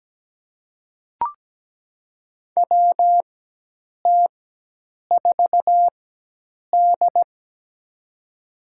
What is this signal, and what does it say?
Morse code, heard as a keyed tone, about 17 wpm, spaced as for 8 wpm, 700 Hz. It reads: WT4D
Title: USA Call Signs - 4 Characters - 4x Character Spacing - 17wpm